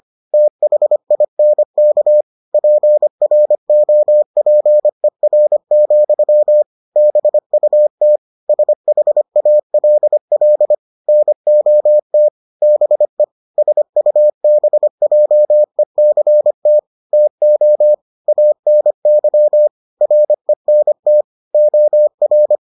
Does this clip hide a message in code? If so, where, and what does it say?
Yes: Morse code from 0.3 to 22.6 s, heard as a keyed tone, about 25 wpm, 600 Hz. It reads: THINK PROPER, BUT SHALL NOT BE SUBJECT TO ANY RENT OR